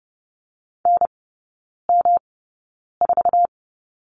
Morse code, heard as a keyed tone, about 30 wpm, spaced as for 9 wpm, 700 Hz. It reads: NM4
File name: Lesson 18e - Letter m - Three Characters 30wpm